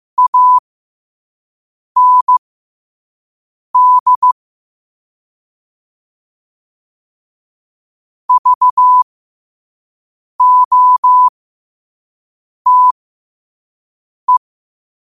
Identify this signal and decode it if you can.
Morse code, heard as a keyed tone, about 15 wpm, spaced as for 5 wpm, 1000 Hz. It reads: AND VOTE
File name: qotdfiles / 5wpm